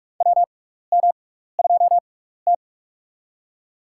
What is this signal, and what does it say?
Morse code, heard as a keyed tone, about 44 wpm, spaced as for 15 wpm, 700 Hz. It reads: WM2T
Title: USA Call Signs - 4 Characters - 6x Character Spacing - 45wpm